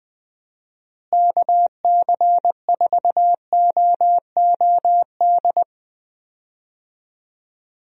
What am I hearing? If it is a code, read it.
Morse code, heard as a keyed tone, about 20 wpm, 700 Hz. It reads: KC4OOD
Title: USA Call Signs - 6 Characters - Repeated 3x - 20wpm